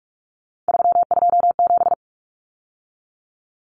Morse code, heard as a keyed tone, about 44 wpm, 700 Hz. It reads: 327
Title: Common Cut Numbers - 3-Digit 45wpm